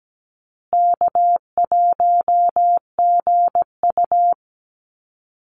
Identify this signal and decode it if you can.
Morse code, heard as a keyed tone, about 17 wpm, 700 Hz. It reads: K1GU
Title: Easy Call Signs Repeated 2x - 17wpm